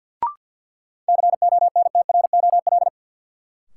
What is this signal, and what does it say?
Morse code, heard as a keyed tone, about 53 wpm, 700 Hz. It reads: CONTROL
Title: Sentences from Top 1000 Words - Review Part 2 - New Word Drill - 50wpm